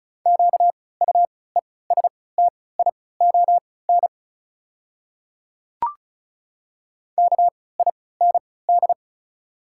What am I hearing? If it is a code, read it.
Morse code, heard as a keyed tone, about 34 wpm, spaced as for 20 wpm, 690 Hz. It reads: QUESTION KIND
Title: 3x Character Spacing - Top 500 Words - 35wpm